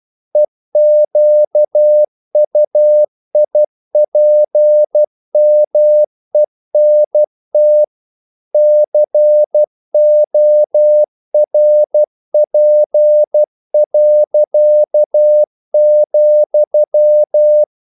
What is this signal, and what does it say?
Morse code, heard as a keyed tone, about 12 wpm, 600 Hz. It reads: EQUIPMENT CORP.,
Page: English quotations in CW (telegraphy - morse)